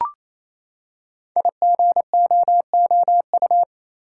Morse code, heard as a keyed tone, about 29 wpm, 700 Hz. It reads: IGOOU
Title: Code Groups - 5 Letters - 28wpm